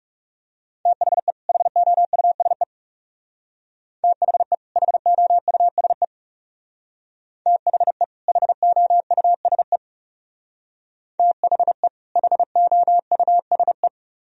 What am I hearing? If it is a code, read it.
Morse code, heard as a keyed tone, about 36 wpm, 700 Hz. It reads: THE HOUSE THE HOUSE THE HOUSE THE HOUSE
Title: Speed-Racing - Sets of 2 Words 30wpm